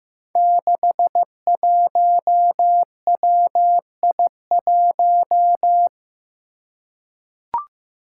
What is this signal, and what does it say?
Morse code, heard as a keyed tone, about 15 wpm, 700 Hz. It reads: 61WI1
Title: Five Challenging Letter-Number Combinations 15wpm